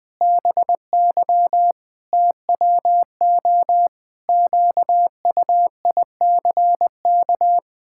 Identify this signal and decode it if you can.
Morse code, heard as a keyed tone, about 21 wpm, 700 Hz. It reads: BY TWO QUICK
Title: Pangram Challenge v2 - 20wpm